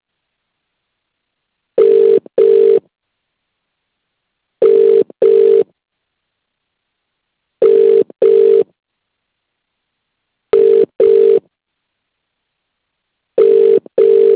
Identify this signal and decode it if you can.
Ringback tone (UK), 400 + 450 Hz.